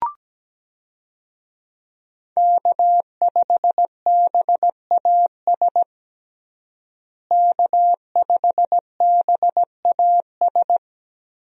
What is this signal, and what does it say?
Morse code, heard as a keyed tone, about 17 wpm, 700 Hz. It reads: K5BAS K5BAS
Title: USA Call Signs - 5 Characters - Repeated 3x - 17wpm